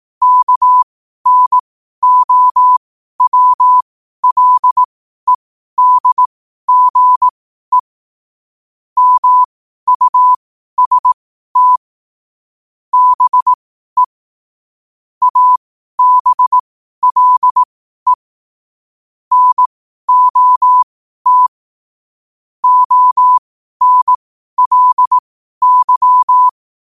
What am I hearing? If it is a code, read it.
Morse code, heard as a keyed tone, about 18 wpm, spaced as for 12 wpm, 1000 Hz. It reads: KNOWLEDGE MUST BE ABLE NOT ONLY
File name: qotdfiles / 12wpm